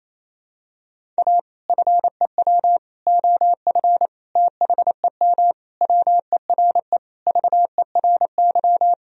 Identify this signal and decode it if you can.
Morse code, heard as a keyed tone, about 28 wpm, 700 Hz. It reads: A FEW OF THEM WERE VERY
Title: Sentences from Top 1000 Words - Review Part 1 - 28wpm